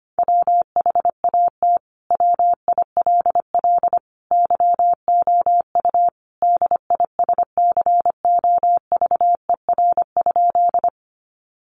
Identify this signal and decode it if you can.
Morse code, heard as a keyed tone, about 25 wpm, 700 Hz. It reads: WHAT WILL YOU DISCOVER?
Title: Sentences from Top 1100 Words 25wpm